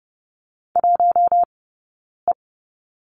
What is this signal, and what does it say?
Morse code, heard as a keyed tone, about 30 wpm, spaced as for 9 wpm, 690 Hz. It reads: 1E
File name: Lesson 19d - Letter w - Two Characters 30wpm